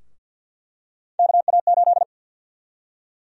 Morse code, heard as a keyed tone, about 48 wpm, 700 Hz. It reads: KA8
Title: USA Call Sign Prefixes - Repeated 2x - 50wpm